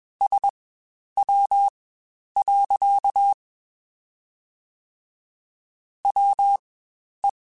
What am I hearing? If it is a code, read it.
Morse code, heard as a keyed tone, about 21 wpm, spaced as for 10 wpm, 800 Hz. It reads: SW. WE
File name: Spaced Characters / Spaced Characters-mp3s / Weather5 5 sp @20